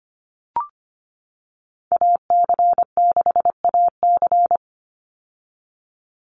Morse code, heard as a keyed tone, about 25 wpm, 700 Hz. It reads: AC6AC
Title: Call Signs - 25wpm